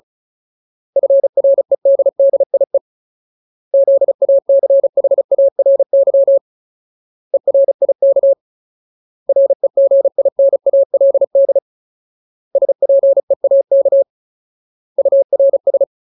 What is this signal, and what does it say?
Morse code, heard as a keyed tone, about 35 wpm, 550 Hz. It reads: FREDDIE ZACHARY ERIK REGINALD SPEAK URS
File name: Ultimate Word List / 35WPM Ultimate